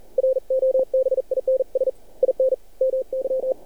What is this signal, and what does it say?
Morse code, heard as a keyed tone, about 46 wpm, 520 Hz. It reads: ROBINS IN MY
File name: archived / CRSnet